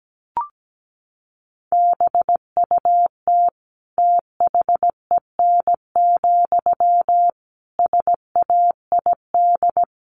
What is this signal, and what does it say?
Morse code, heard as a keyed tone, about 17 wpm, 700 Hz. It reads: BUT THEN, SAID
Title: Sentences from Top 1500 Words 17wpm